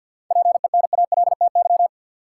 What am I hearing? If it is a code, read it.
Morse code, heard as a keyed tone, about 53 wpm, 700 Hz. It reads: PENALTY
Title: Seven Letter Words - 50wpm